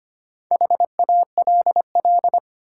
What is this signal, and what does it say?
Morse code, heard as a keyed tone, about 25 wpm, 700 Hz. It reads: HALL
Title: Four Letter Words - 25wpm